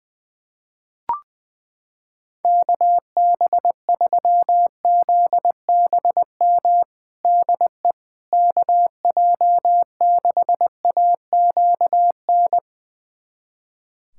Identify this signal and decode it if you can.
Morse code, heard as a keyed tone, about 21 wpm, 700 Hz. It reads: KB3ZBM DE KJ6AQN
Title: USA Callsign DE Callsign - 6 Characters - 20wpm